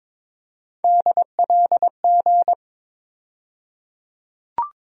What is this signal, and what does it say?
Morse code, heard as a keyed tone, about 22 wpm, 700 Hz. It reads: DLG
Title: USA Call Sign Postfixes - 3 Letters 22wpm